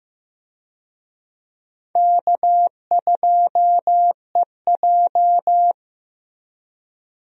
Morse code, heard as a keyed tone, about 15 wpm, 700 Hz. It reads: K2EJ
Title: USA Call Signs - 4 Characters - Repeated 3x - 15wpm